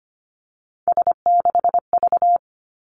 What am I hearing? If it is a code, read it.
Morse code, heard as a keyed tone, about 25 wpm, 700 Hz. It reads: S6V
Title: Three Challenging Letter-Number Combinations 25wpm